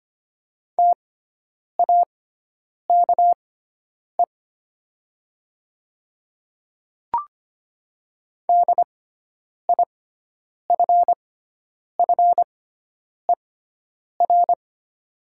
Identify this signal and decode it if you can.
Morse code, heard as a keyed tone, about 25 wpm, spaced as for 9 wpm, 700 Hz. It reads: TAKE DIFFER
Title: 6x Character Spacing - Top 200 Words - 25wpm